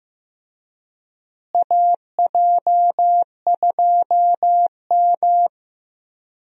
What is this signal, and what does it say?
Morse code, heard as a keyed tone, about 15 wpm, 700 Hz. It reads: AJ2M
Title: USA Call Signs - 4 Characters - Repeated 3x - 15wpm